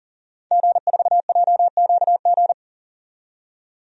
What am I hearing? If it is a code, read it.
Morse code, heard as a keyed tone, about 38 wpm, 700 Hz. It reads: G4JQG
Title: DXCC-100 Call Signs - 5 Characters - 40wpm